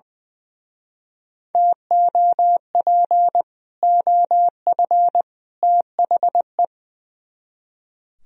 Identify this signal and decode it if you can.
Morse code, heard as a keyed tone, about 22 wpm, 700 Hz. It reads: TOP OF THE
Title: Speed-Racing - Sets of 3 Words 20wpm